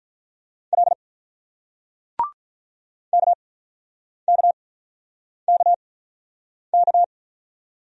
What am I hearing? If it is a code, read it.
Morse code, heard as a keyed tone, about 46 wpm, spaced as for 9 wpm, 690 Hz. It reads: L KKKK